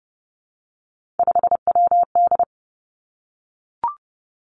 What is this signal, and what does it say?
Morse code, heard as a keyed tone, about 30 wpm, 700 Hz. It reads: 5WD
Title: Three Challenging Letter-Number Combinations 30wpm